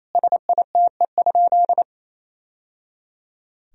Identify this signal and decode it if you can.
Morse code, heard as a keyed tone, about 28 wpm, 700 Hz. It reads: SITE?
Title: Sentences from Top 1000 Words - Review Part 4 - 28wpm